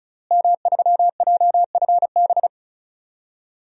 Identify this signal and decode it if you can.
Morse code, heard as a keyed tone, about 34 wpm, 700 Hz. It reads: M3JFB
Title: DXCC-100 Call Signs - 5 Characters - Repeated 2x - 35wpm